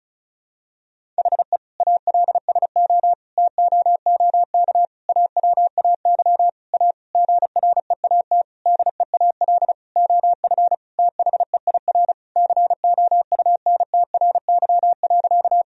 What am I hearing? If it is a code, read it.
Morse code, heard as a keyed tone, about 36 wpm, 700 Hz. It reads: HE ALSO TOOK AWAY A GREAT DEAL OF THEIR COUNTRY.
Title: Sentences from Top 1000 Words - Review Part 1 - 35wpm